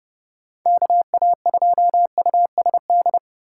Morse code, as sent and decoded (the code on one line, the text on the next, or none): -.- .- ..--- ..- ... -..
KA2USD